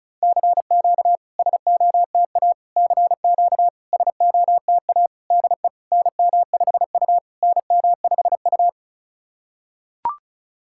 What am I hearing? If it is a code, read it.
Morse code, heard as a keyed tone, about 34 wpm, 690 Hz. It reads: CQ SOTA CQ SOTA DE NM5U NM5U
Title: SOTA QSO Practice 35wpm